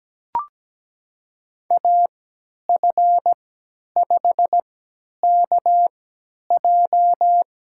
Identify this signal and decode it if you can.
Morse code, heard as a keyed tone, about 17 wpm, spaced as for 10 wpm, 700 Hz. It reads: AF5KJ
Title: USA Call Signs - 5 Characters - 3x Character Spacing - 17wpm